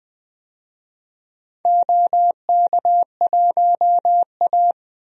Morse code, heard as a keyed tone, about 20 wpm, 700 Hz. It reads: OK1A
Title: DXCC-100 Call Signs - 4 Characters - 20wpm